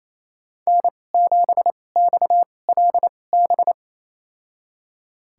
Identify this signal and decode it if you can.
Morse code, heard as a keyed tone, about 28 wpm, spaced as for 20 wpm, 700 Hz. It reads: N7XLB
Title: USA Call Signs - 5 Characters - 2x Character Spacing - 28wpm